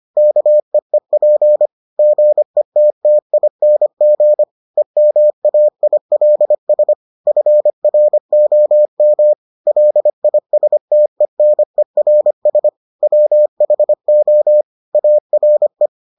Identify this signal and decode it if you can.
Morse code, heard as a keyed tone, about 25 wpm, 600 Hz. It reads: KEEP GETTING EMAILS FROM LISTENERS WHO ARE